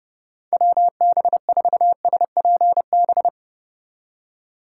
Morse code, heard as a keyed tone, about 30 wpm, 700 Hz. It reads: WB4SPB
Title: K1USN SST QSO Practice - 30wpm